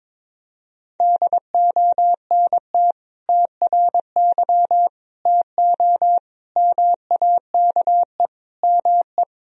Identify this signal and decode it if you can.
Morse code, heard as a keyed tone, about 21 wpm, 700 Hz. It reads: DONT TRY TO MAKE ME
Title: Sentences from Top 1000 Words - Review Part 1 - 22wpm